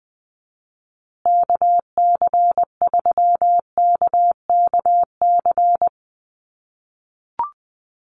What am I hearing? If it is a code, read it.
Morse code, heard as a keyed tone, about 20 wpm, 700 Hz. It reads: KC3KKC
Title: USA Call Signs Repeated 3x 20wpm